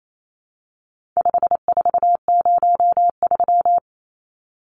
Morse code, heard as a keyed tone, about 28 wpm, 700 Hz. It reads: 5403
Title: Numbers - 4-Digit 28wpm